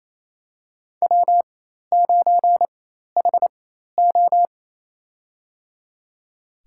Morse code, heard as a keyed tone, about 27 wpm, spaced as for 13 wpm, 700 Hz. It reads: W9HO
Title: USA Call Signs - 4x Character Spacing - 28wpm